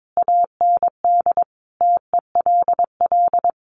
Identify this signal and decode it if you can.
Morse code, heard as a keyed tone, about 22 wpm, 700 Hz. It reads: AND TELL
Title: Sentences from Top 1100 Words 22wpm